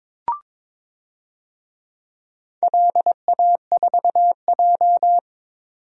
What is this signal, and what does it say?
Morse code, heard as a keyed tone, about 23 wpm, 700 Hz. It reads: LA4J
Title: DXCC-100 Call Signs - 4 Characters - 22wpm